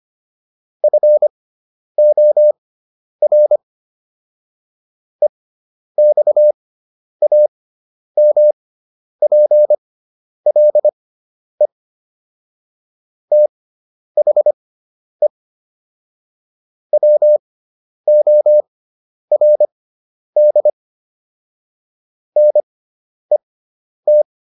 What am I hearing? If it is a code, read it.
Morse code, heard as a keyed tone, about 25 wpm, spaced as for 10 wpm, 600 Hz. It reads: FOR EXAMPLE THE WORD NET